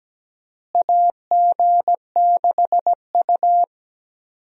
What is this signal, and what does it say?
Morse code, heard as a keyed tone, about 17 wpm, 700 Hz. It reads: AG6U
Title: USA Call Signs - 4 Characters - Repeated 3x - 17wpm